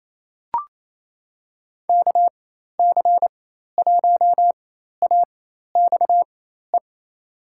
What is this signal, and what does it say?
Morse code, heard as a keyed tone, about 28 wpm, spaced as for 13 wpm, 700 Hz. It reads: KC1AXE
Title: USA Call Signs - 6 Characters - 4x Character Spacing - 28wpm